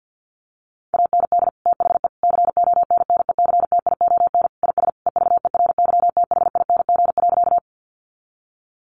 Morse code, heard as a keyed tone, about 48 wpm, 700 Hz. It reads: AND THE CONNECTION IS EVERYTHING.
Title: Sentences from Top 1300 Words 50wpm